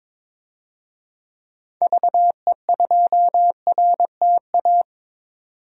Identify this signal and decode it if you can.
Morse code, heard as a keyed tone, about 22 wpm, 700 Hz. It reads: VE2RTA
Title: DXCC-100 Call Signs - 6 Characters - 22wpm